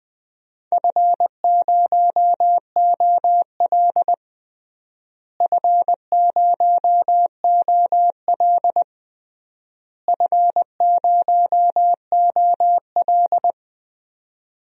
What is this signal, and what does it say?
Morse code, heard as a keyed tone, about 20 wpm, 700 Hz. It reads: F0OL F0OL F0OL
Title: DXCC-100 Call Signs - 4 Characters - Repeated 3x - 20wpm